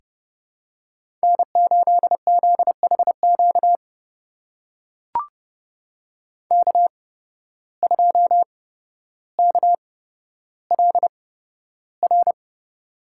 Morse code, heard as a keyed tone, about 31 wpm, spaced as for 13 wpm, 700 Hz. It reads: N8ZHQ K2KLR
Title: USA Call Signs - 5 Characters - 8x Character Spacing - 30wpm